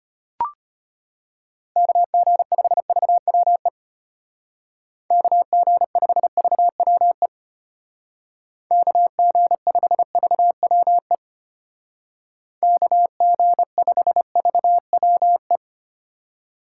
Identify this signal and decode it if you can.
Morse code, heard as a keyed tone, about 31 wpm, 700 Hz. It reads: KG5VWE KG5VWE KG5VWE KG5VWE
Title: Speed Racing - USA Call Signs 25wpm